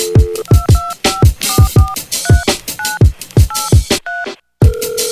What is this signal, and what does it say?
DTMF, heard as telephone touch tones, tones of about 200 ms. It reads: 225443983